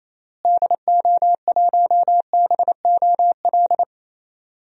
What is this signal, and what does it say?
Morse code, heard as a keyed tone, about 28 wpm, 700 Hz. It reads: DO1BOL
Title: DXCC-100 Call Signs - 6 Characters - Repeated 2x - 28wpm